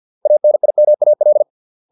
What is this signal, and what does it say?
Morse code, heard as a keyed tone, about 50 wpm, 600 Hz. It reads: ANIMAL